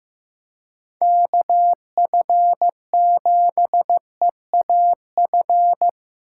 Morse code, heard as a keyed tone, about 15 wpm, 700 Hz. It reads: KF7EAF
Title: USA Call Signs - 6 Characters - 4x Character Spacing - 15wpm